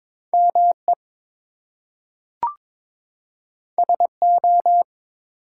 Morse code, heard as a keyed tone, about 22 wpm, 700 Hz. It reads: ME SO
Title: Two Letter Words - 22wpm